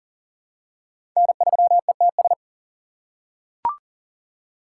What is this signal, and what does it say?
Morse code, heard as a keyed tone, about 40 wpm, 700 Hz. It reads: N3ETS
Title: USA Call Signs - 5 Characters - 2x Character Spacing - 40wpm